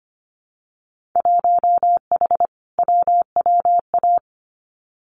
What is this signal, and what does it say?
Morse code, heard as a keyed tone, about 25 wpm, 690 Hz. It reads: 1H WWA
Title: Winter Field Day Exchanges 25wpm